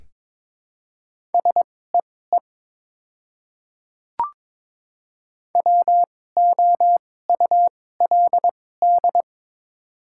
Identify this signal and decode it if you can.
Morse code, heard as a keyed tone, about 22 wpm, spaced as for 16 wpm, 690 Hz. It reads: SEE WOULD
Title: 2x Character Spacing - Top 100 Words - 22wpm